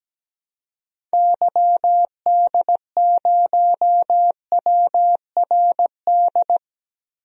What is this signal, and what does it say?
Morse code, heard as a keyed tone, about 17 wpm, 700 Hz. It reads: YD0WRD
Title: DXCC-100 Call Signs - 6 Characters - 17wpm